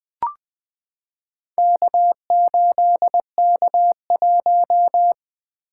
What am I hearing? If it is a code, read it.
Morse code, heard as a keyed tone, about 20 wpm, 700 Hz. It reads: K8K1
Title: Four Letter-Number Combinations - 20wpm